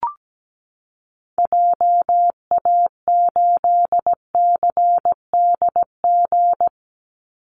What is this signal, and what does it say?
Morse code, heard as a keyed tone, about 17 wpm, 700 Hz. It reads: JA8CDG
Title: Call Signs Repeated 2x - 17wpm